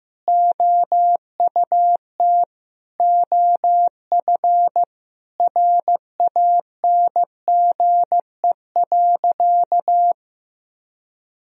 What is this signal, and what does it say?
Morse code, heard as a keyed tone, about 16 wpm, 700 Hz. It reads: OUT OF RANGE.
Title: Sentences from Top 1000 Words - Review Part 4 - 15wpm